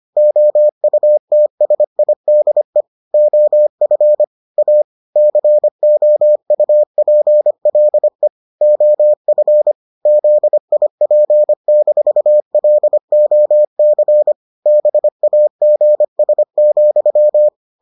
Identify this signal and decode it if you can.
Morse code, heard as a keyed tone, about 25 wpm, 600 Hz. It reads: OUTSIDE OF A COUPLE OF ZIP-LOC BAGS,